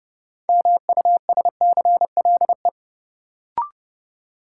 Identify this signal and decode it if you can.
Morse code, heard as a keyed tone, about 30 wpm, 700 Hz. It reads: MUSCLE
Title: Sentences from Top 1300 Words - New Word Drill - 30wpm